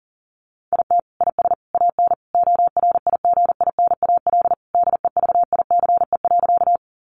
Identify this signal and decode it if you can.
Morse code, heard as a keyed tone, about 40 wpm, 700 Hz. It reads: IT IS AN ORIGINAL DEVICE.
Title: Sentences from Top 1500 Words 40wpm